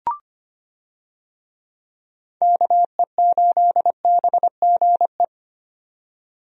Morse code, heard as a keyed tone, about 25 wpm, 700 Hz. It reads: KE8BGE
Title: USA Call Signs - 6 Characters - 25wpm